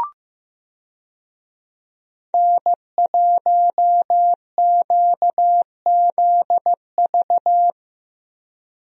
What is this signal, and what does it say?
Morse code, heard as a keyed tone, about 16 wpm, 700 Hz. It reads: N1QZV
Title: USA Call Signs - 5 Characters - Repeated 2x - 15wpm